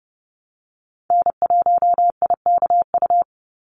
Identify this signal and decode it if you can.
Morse code, heard as a keyed tone, about 32 wpm, 700 Hz. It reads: N1IKU
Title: USA Call Signs 30wpm